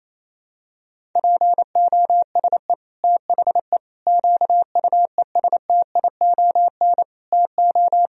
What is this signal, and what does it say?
Morse code, heard as a keyed tone, about 28 wpm, 700 Hz. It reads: POSE THE QUESTION TO
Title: Sentences from Top 1000 Words - Review Part 2 - 28wpm